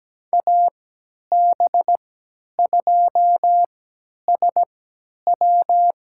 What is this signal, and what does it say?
Morse code, heard as a keyed tone, about 17 wpm, spaced as for 10 wpm, 700 Hz. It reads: AB2SW
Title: USA Call Signs - 5 Characters - 3x Character Spacing - 17wpm